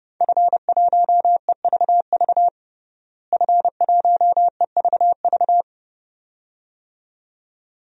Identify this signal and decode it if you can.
Morse code, heard as a keyed tone, about 30 wpm, 700 Hz. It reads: F1EVV F1EVV
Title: DXCC-100 Call Signs - 5 Characters - Repeated 2x - 30wpm